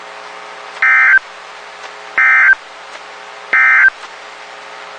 Emergency Alert System (SAME) tones, heard as harsh buzzing bursts. It is heard from 0.7 to 4.0 s.
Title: WXK91 NOAA Weather Radio